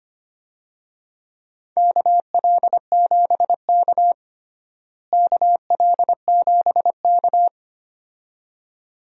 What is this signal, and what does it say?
Morse code, heard as a keyed tone, about 25 wpm, 700 Hz. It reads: KL7K KL7K
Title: DXCC-100 Call Signs - 4 Characters - Repeated 2x - 25wpm